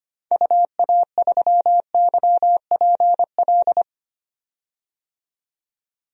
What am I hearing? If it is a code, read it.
Morse code, heard as a keyed tone, about 25 wpm, 700 Hz. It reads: UA3YPL
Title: DXCC-100 Call Signs - 6 Characters - 25wpm